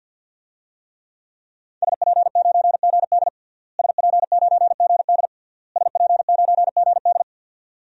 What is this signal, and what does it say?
Morse code, heard as a keyed tone, about 49 wpm, 700 Hz. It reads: SP9GD SP9GD SP9GD
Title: DXCC-100 Call Signs - 5 Characters - Repeated 3x - 50wpm